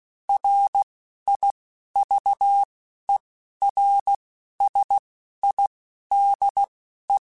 Morse code, heard as a keyed tone, about 16 wpm, spaced as for 11 wpm, 800 Hz. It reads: RIVERSIDE